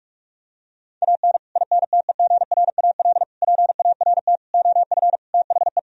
Morse code, heard as a keyed tone, about 49 wpm, 700 Hz. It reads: AN INTEGRAL PART OF THE